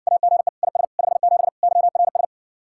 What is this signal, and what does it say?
Morse code, heard as a keyed tone, about 60 wpm, 700 Hz. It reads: AGE IS 57 YRS